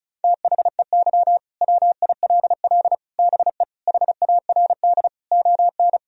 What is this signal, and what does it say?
Morse code, heard as a keyed tone, about 34 wpm, 700 Hz. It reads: THEY WILL BE HARD ON
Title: Sentences from Top 1000 Words - Review Part 1 - 35wpm